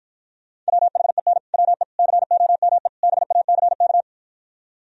Morse code, heard as a keyed tone, about 55 wpm, 700 Hz. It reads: WHEN WE COME BACK